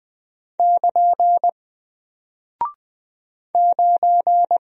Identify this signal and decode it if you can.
Morse code, heard as a keyed tone, about 20 wpm, 700 Hz. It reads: <KN> 9
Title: Single Character (Letter:Number:Punctuation:Prosign) Alt 20wpm